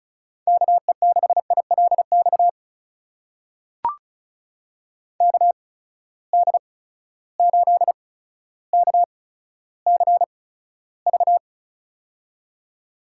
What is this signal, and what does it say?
Morse code, heard as a keyed tone, about 36 wpm, spaced as for 15 wpm, 700 Hz. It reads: KE6ILX KD8KCV
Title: USA Call Signs - 6 Characters - 8x Character Spacing - 35wpm